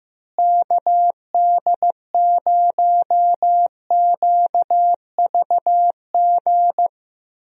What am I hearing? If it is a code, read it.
Morse code, heard as a keyed tone, about 15 wpm, 700 Hz. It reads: KD0QVG
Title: USA Call Signs - 6 Characters - Repeated 3x - 15wpm